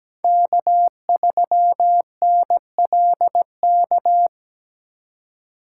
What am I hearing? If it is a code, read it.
Morse code, heard as a keyed tone, about 17 wpm, 700 Hz. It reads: K3NLK
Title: USA Call Signs - 5 Characters - 2x Character Spacing - 17wpm